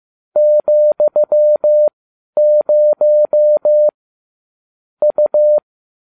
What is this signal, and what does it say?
Morse code, heard as a keyed tone, about 15 wpm, spaced as for 11 wpm, 600 Hz. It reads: ,0 U